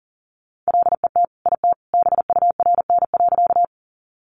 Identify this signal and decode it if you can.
Morse code, heard as a keyed tone, about 40 wpm, 700 Hz. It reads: LET IT BURN.
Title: Sentences from Top 1300 Words 40wpm